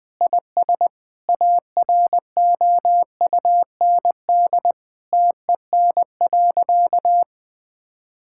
Morse code, heard as a keyed tone, about 20 wpm, 700 Hz. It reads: IS AROUND TEN.
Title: Sentences with 6-Word Phrases - Spoken Beginning - 20wpm